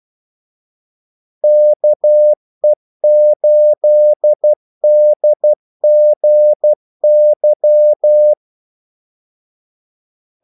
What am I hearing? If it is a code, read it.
Morse code, heard as a keyed tone, about 12 wpm, 600 Hz. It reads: KE8DGY
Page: 2 ekstra ordmellemrum mellem kaldesignalerne